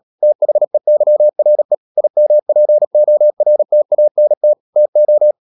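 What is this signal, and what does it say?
Morse code, heard as a keyed tone, about 37 wpm, 600 Hz. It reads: THEYRE IMPORTANT TO
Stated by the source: Morse practice at 36 wpm, text of Ken Follett's novel Never, from CW med Gnister.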